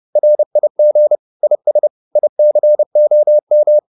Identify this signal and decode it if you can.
Morse code, heard as a keyed tone, about 30 wpm, 600 Hz. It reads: RIG IS ICOM
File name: QSO206 30